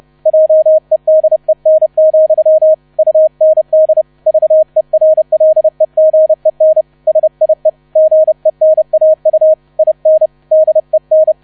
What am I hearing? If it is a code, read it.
Morse code, heard as a keyed tone, about 31 wpm, 630 Hz. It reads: JEDEN, UND VERLEGEN SIE GENAU IN DEN